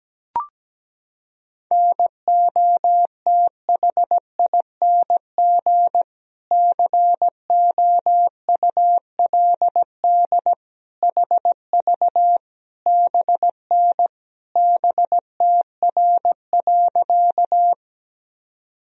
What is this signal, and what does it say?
Morse code, heard as a keyed tone, about 17 wpm, 700 Hz. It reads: NOTHING COULD HV BN BTR.
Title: Sentences from Top 500 Words and Abbreviations 17wpm